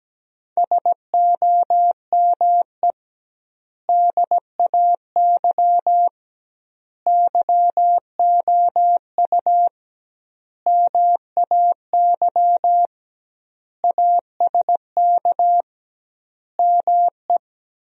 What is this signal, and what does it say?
Morse code, heard as a keyed tone, about 17 wpm, 700 Hz. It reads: SOME DAY YOU MAY ASK ME
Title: 2x Word-Spacing - Sentences from Top 200 Words - 17wpm